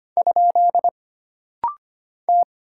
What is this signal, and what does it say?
Morse code, heard as a keyed tone, about 25 wpm, 700 Hz. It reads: ? T
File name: Single Character (Letter:Number:Punctuation:Prosign) - Rapid-Fire - 25wpm